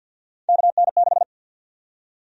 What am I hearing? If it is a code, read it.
Morse code, heard as a keyed tone, about 52 wpm, 700 Hz. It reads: KN6
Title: USA Call Sign Prefixes - Repeated 2x - 50wpm